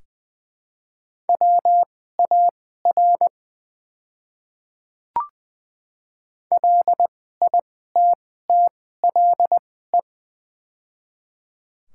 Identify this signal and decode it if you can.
Morse code, heard as a keyed tone, about 20 wpm, spaced as for 14 wpm, 700 Hz. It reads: WAR LITTLE
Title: 2x Character Spacing - Top 500 Words - 20wpm